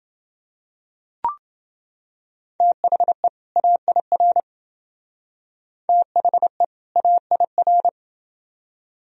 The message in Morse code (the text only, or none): THE AIR THE AIR